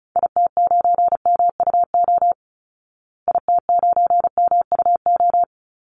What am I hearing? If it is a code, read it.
Morse code, heard as a keyed tone, about 34 wpm, 700 Hz. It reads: IT9MUO IT9MUO
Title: Call Signs Repeated 2x - 35wpm